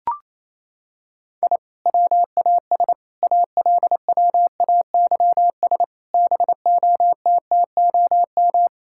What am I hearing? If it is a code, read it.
Morse code, heard as a keyed tone, about 28 wpm, 700 Hz. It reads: I WAS ALWAYS BOTTOM
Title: Sentences from Top 1000 Words - Review Part 6 - 28wpm